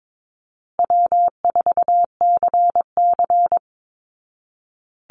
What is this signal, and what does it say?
Morse code, heard as a keyed tone, about 22 wpm, 700 Hz. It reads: W4CC
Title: Easy Call Signs Repeated 3x - 22wpm